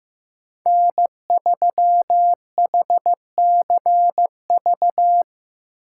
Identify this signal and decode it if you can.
Morse code, heard as a keyed tone, about 15 wpm, 700 Hz. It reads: N3HCV
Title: USA Call Signs - 5 Characters - 4x Character Spacing - 15wpm